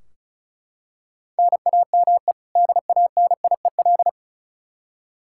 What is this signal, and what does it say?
Morse code, heard as a keyed tone, about 34 wpm, 700 Hz. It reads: NAME DANIEL
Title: Practice QSO - Exchange 1 - Beginning - 35wpm